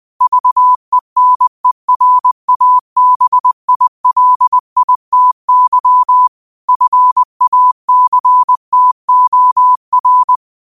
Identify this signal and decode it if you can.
Morse code, heard as a keyed tone, about 20 wpm, 1000 Hz. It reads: VENERABILITY FACTOR